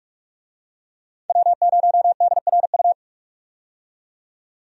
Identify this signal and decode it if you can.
Morse code, heard as a keyed tone, about 46 wpm, 700 Hz. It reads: W0DRU
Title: USA Call Signs - 5 Characters - 45wpm